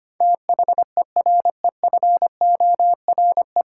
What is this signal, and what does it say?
Morse code, heard as a keyed tone, about 25 wpm, 700 Hz. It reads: THEREFORE
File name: Nine Letter Words - 25wpm